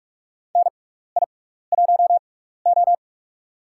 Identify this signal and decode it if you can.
Morse code, heard as a keyed tone, about 44 wpm, spaced as for 15 wpm, 700 Hz. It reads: NI1O
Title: USA Call Signs - 4 Characters - 6x Character Spacing - 45wpm